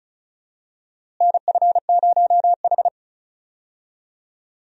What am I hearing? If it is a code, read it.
Morse code, heard as a keyed tone, about 36 wpm, 700 Hz. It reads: NF0H
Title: USA Call Signs - 4 Characters - Repeated 2x - 35wpm